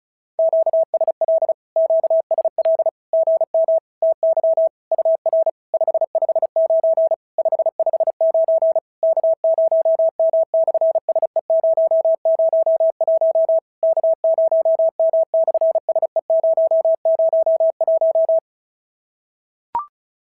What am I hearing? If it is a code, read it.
Morse code, heard as a keyed tone, about 34 wpm, 650 Hz. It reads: QSL QSL GM TY UR 559 559 K0M/SE001 K0M/SE001
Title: SOTA QSO Practice 35wpm